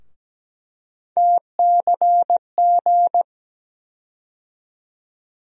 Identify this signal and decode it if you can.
Morse code, heard as a keyed tone, about 17 wpm, 700 Hz. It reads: TCG
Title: USA Call Sign Postfixes - 3 Letters 17wpm